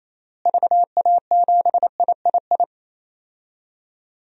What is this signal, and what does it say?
Morse code, heard as a keyed tone, about 28 wpm, 700 Hz. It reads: VA7III
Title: DXCC-100 Call Signs - 6 Characters - Repeated 3x - 28wpm